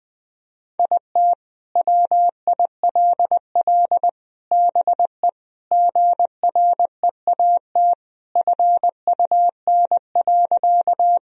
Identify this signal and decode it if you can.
Morse code, heard as a keyed tone, about 20 wpm, 700 Hz. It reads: IT WILL BE GREAT FUN.
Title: Sentences from Top 1000 Words - Review Part 4 - 20wpm